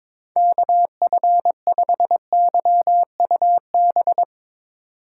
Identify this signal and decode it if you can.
Morse code, heard as a keyed tone, about 22 wpm, 700 Hz. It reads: KF5YUB
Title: K1USN SST QSO Practice - 22wpm